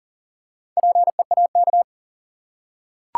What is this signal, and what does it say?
Morse code, heard as a keyed tone, about 40 wpm, 700 Hz. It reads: PEAK
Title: Four Letter Words - 40wpm